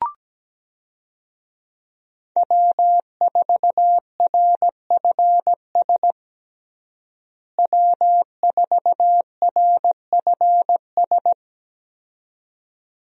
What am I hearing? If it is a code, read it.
Morse code, heard as a keyed tone, about 17 wpm, 700 Hz. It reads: W4RFS W4RFS
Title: USA Call Signs - 5 Characters - Repeated 2x - 17wpm